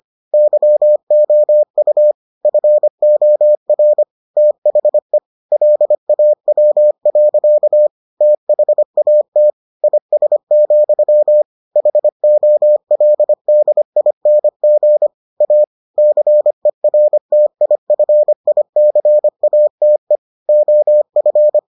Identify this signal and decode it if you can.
Morse code, heard as a keyed tone, about 25 wpm, 600 Hz. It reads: YOU FOR THE LAW. THAT IS, HOLDING A CERTIFICATE OF